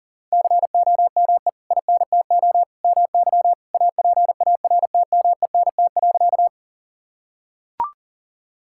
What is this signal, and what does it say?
Morse code, heard as a keyed tone, about 40 wpm, 700 Hz. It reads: COME INTO MY APARTMENT.
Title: Sentences from Top 1000 Words - Review Part 6 - 40wpm